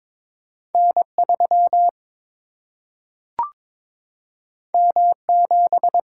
Morse code, heard as a keyed tone, about 22 wpm, 700 Hz. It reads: N 3 M 7
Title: Single-Letter Single-Number 22wpm